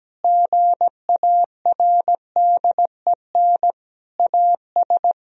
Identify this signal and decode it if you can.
Morse code, heard as a keyed tone, about 17 wpm, 700 Hz. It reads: GARDEN AS
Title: Sentences from Top 1000 Words - Review Part 4 - 17wpm